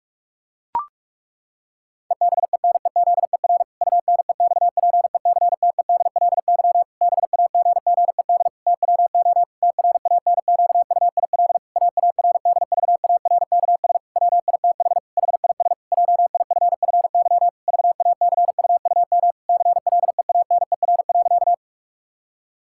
Morse code, heard as a keyed tone, about 41 wpm, 700 Hz. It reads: EBENEZER UNEXPECTEDLY BAGGED TWO TRANQUIL AARDVARKS WITH HIS JIFFY VACUUM CLEANER.
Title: Pangram Challenge v2 - 45wpm